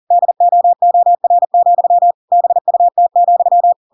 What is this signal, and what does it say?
Morse code, heard as a keyed tone, about 40 wpm, 700 Hz. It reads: DOOR, BUT,
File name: Pinocchio / Pinocchio ch7 40 40 ik1hge